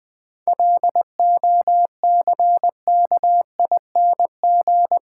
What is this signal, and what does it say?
Morse code, heard as a keyed tone, about 20 wpm, 700 Hz. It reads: LOCKING
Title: Difficult Sentences with Hints - 20wpm